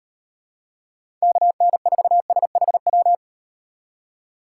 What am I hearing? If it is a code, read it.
Morse code, heard as a keyed tone, about 36 wpm, 700 Hz. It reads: KN4SHW